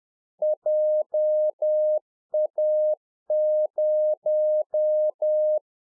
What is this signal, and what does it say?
Morse code, heard as a keyed tone, about 10 wpm, 610 Hz. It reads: JA0